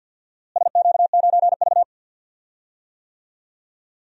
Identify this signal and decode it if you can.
Morse code, heard as a keyed tone, about 46 wpm, 700 Hz. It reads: SQ9V
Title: DXCC-100 Call Signs - 4 Characters - Repeated 2x - 50wpm